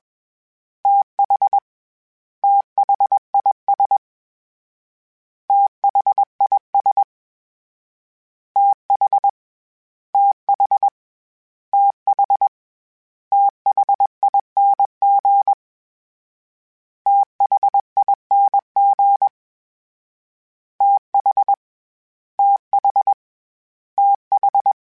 Morse code, heard as a keyed tone, about 21 wpm, 800 Hz. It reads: TH THIS THIS TH TH TH THING THING TH TH TH
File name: THs / TH / TH Prefixes 3 @20